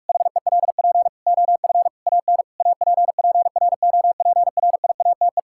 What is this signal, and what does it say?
Morse code, heard as a keyed tone, about 41 wpm, 700 Hz. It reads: HELP OF AN APPROPRIATE
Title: Sentences from Top 1000 Words - Review Part 5 - 45wpm